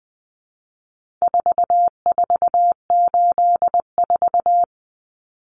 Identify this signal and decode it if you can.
Morse code, heard as a keyed tone, about 20 wpm, 700 Hz. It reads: 4484
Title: Common Cut Numbers - 4-Digit 20wpm